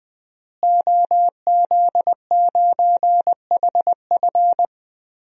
Morse code, heard as a keyed tone, about 20 wpm, 700 Hz. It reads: OZ9HF
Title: DXCC-100 Call Signs - 5 Characters - Repeated 3x - 20wpm